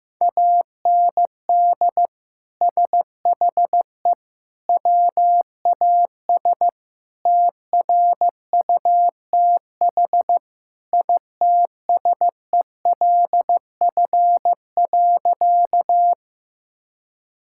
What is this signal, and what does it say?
Morse code, heard as a keyed tone, about 15 wpm, 700 Hz. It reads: AND SHE WAS TRUTH ITSELF.
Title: Sentences from Top 1000 Words - Review Part 5 - 15wpm